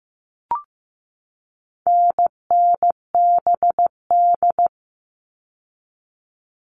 Morse code, heard as a keyed tone, about 15 wpm, 700 Hz. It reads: NNBD
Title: All Cut Numbers - 4-Digit 15wpm